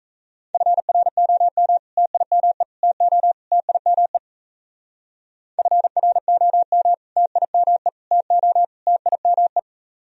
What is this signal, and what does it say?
Morse code, heard as a keyed tone, about 34 wpm, 700 Hz. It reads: FROM TIME TO TIME FROM TIME TO TIME
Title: Speed-Racing - Sets of 4 Words 28wpm